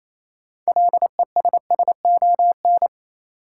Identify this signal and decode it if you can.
Morse code, heard as a keyed tone, about 28 wpm, 700 Hz. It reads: LESSON
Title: Sentences from Top 1200 Words - New Word Drill - 28wpm